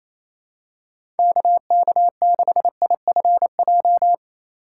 Morse code, heard as a keyed tone, about 28 wpm, 700 Hz. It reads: KK6IFJ
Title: USA Call Signs - 6 Characters - 4x Character Spacing - 28wpm